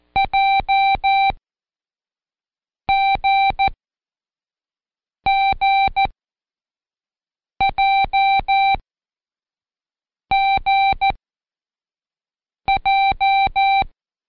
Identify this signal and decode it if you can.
Morse code, heard as a keyed tone, about 14 wpm, spaced as for 5 wpm, 780 Hz. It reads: JGGJGJ
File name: Code Quick / AudioDownld1 / Seg56-Day6-Pract JGXQ